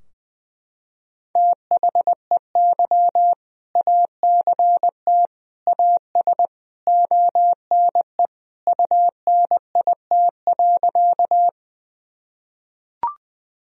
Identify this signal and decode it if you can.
Morse code, heard as a keyed tone, about 20 wpm, 700 Hz. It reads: THEY ACT AS ONE UNIT.
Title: Sentences from Top 1000 Words - Review Part 2 - 20wpm